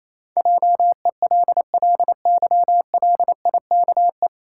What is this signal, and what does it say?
Morse code, heard as a keyed tone, about 28 wpm, 700 Hz. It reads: JELLYLIKE
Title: Pangram Challenge v2 - 28wpm